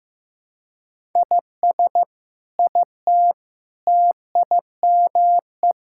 Morse code, heard as a keyed tone, about 15 wpm, 700 Hz. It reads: IS IT TIME
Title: Sentences from Top 1000 Words - Review Part 6 - 15wpm